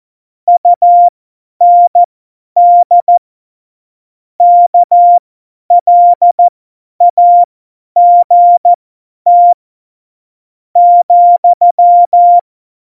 Morse code, heard as a keyed tone, about 14 wpm, spaced as for 9 wpm, 700 Hz. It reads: UND KLAGT,
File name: Morsekurs / Witze / Witz 20 March 2018, 50 F70